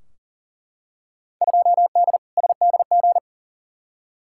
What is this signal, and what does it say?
Morse code, heard as a keyed tone, about 40 wpm, 700 Hz. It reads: 2D SDG
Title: Field Day Exchanges Encore - 40wpm